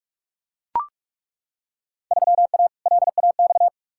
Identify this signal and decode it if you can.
Morse code, heard as a keyed tone, about 42 wpm, 700 Hz. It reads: 3A LAX
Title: Field Day Exchanges - 45wpm